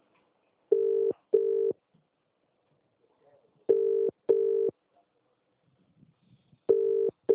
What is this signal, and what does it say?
Ringback tone (UK), 400 + 450 Hz.